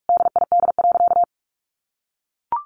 Morse code, heard as a keyed tone, about 44 wpm, 700 Hz. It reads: DID.
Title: Sentences from Top 1400 Words 45wpm